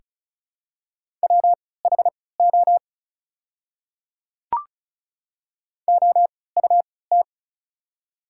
Morse code, heard as a keyed tone, about 36 wpm, spaced as for 20 wpm, 700 Hz. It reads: WHO OUT
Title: 3x Character Spacing - Top 300 Words - 35wpm